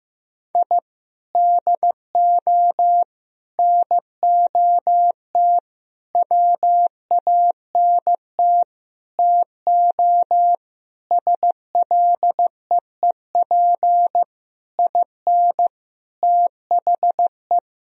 Morse code, heard as a keyed tone, about 15 wpm, 700 Hz. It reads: I DO NOT WANT TO SLEEP IN THE
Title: Sentences from Top 1000 Words - Review Part 2 - 15wpm